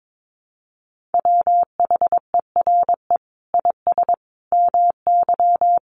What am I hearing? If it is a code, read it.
Morse code, heard as a keyed tone, about 22 wpm, 700 Hz. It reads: WHERE IS MY
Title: Sentences from Top 1400 Words 22wpm